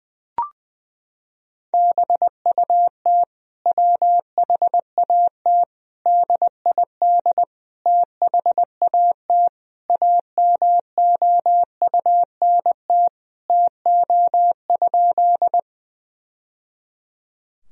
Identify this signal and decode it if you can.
Morse code, heard as a keyed tone, about 20 wpm, 700 Hz. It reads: BUT WHAT DID THAT AMOUNT TO?
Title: Sentences from Top 1000 Words - Review Part 4 - 20wpm